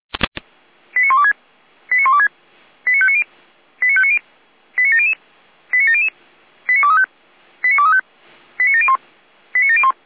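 Five-tone selective call (ZVEI), heard as quick runs of five tones, tones of about 70 ms. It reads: 89126 89126 89590 89590 89700 89700 89235 89235 89701 89701
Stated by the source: ZVEI five-tone call received by SDR.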